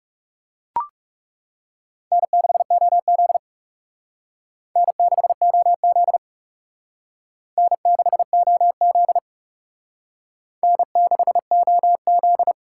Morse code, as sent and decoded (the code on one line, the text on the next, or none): -. -.... --- --.. / -. -.... --- --.. / -. -.... --- --.. / -. -.... --- --..
N6OZ N6OZ N6OZ N6OZ